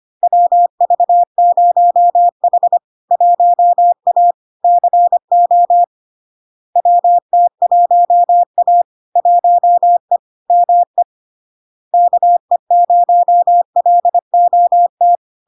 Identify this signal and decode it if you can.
Morse code, heard as a keyed tone, about 25 wpm, 700 Hz. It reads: WV0H 1A CO WT1A 1E ME KE0LOT
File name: Log-in-CW-25wpm-700-Hz